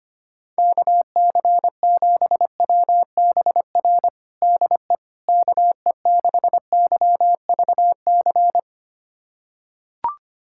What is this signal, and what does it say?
Morse code, heard as a keyed tone, about 25 wpm, 700 Hz. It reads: KC7WBR DE KE6YVC
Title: USA Callsign DE Callsign - 6 Characters - 25wpm